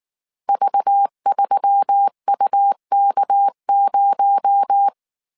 Morse code, heard as a keyed tone, about 19 wpm, 800 Hz. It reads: V3UK0